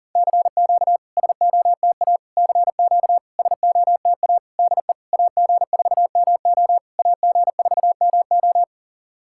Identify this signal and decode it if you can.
Morse code, heard as a keyed tone, about 40 wpm, 700 Hz. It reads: CQ SOTA CQ SOTA DE AG4MO AG4MO
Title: SOTA QSO Practice 40wpm